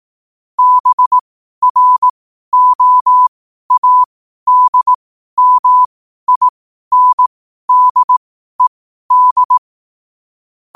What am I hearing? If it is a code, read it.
Morse code, heard as a keyed tone, about 18 wpm, spaced as for 12 wpm, 1000 Hz. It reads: BROADMINDED
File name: qotdfiles / 12wpm